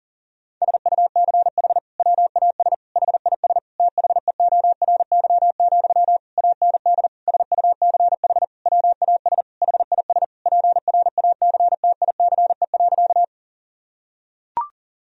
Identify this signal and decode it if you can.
Morse code, heard as a keyed tone, about 40 wpm, 690 Hz. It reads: SUCH WAS HIS THEORY, AND SUCH WAS HIS PRACTICE.
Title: Sentences from Top 1000 Words - Review Part 4 - 40wpm